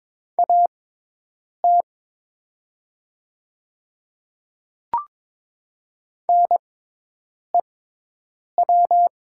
Morse code, heard as a keyed tone, about 22 wpm, spaced as for 8 wpm, 700 Hz. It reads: AT NEW
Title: 6x Character Spacing - Top 200 Words - 22wpm